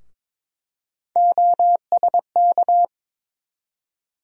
Morse code, heard as a keyed tone, about 22 wpm, 700 Hz. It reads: OSK
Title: USA Call Sign Postfixes - 3 Letters 22wpm